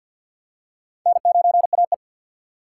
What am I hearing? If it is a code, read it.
Morse code, heard as a keyed tone, about 50 wpm, 700 Hz. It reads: N9AE
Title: USA Call Signs - 6x Character Spacing - 50wpm